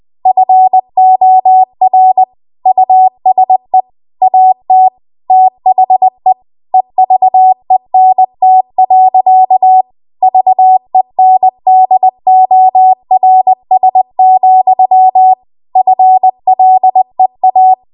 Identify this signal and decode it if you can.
Morse code, heard as a keyed tone, about 20 wpm, 750 Hz. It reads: FOR USE AT THE EVENT. VENDORS, FLEA